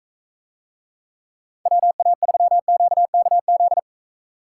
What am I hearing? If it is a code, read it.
Morse code, heard as a keyed tone, about 42 wpm, 700 Hz. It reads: WA3QKZ